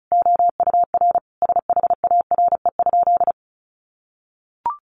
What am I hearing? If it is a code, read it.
Morse code, heard as a keyed tone, about 34 wpm, 700 Hz. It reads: OUR SHARE?
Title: Sentences from Top 900 Words 35wpm